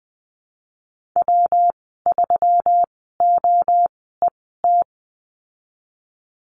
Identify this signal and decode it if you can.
Morse code, heard as a keyed tone, about 20 wpm, spaced as for 15 wpm, 700 Hz. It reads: W3OET
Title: 122 Lesson 20g - Number 3 - Call Signs 20wpm